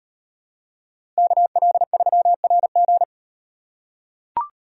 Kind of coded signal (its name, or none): Morse code